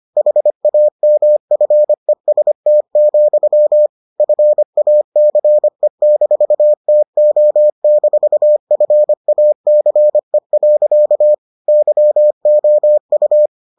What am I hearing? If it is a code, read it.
Morse code, heard as a keyed tone, about 25 wpm, 600 Hz. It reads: HAMFEST, FACE-TO-FACE. YOU